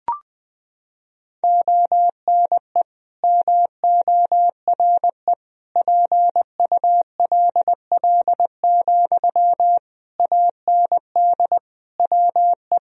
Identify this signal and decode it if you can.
Morse code, heard as a keyed tone, about 20 wpm, 700 Hz. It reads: ONE MORE PULL, AND WE
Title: Sentences from Top 1000 Words - Review Part 2 - 20wpm